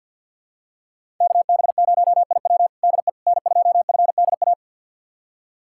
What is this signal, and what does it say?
Morse code, heard as a keyed tone, about 50 wpm, 700 Hz. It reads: KB0IW DE N2FDA
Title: USA Callsign DE Callsign 50wpm